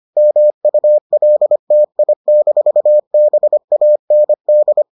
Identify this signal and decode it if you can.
Morse code, heard as a keyed tone, about 25 wpm, 600 Hz. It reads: MULTI-BAND